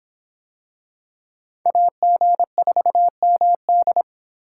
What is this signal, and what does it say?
Morse code, heard as a keyed tone, about 26 wpm, 700 Hz. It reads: AG4MD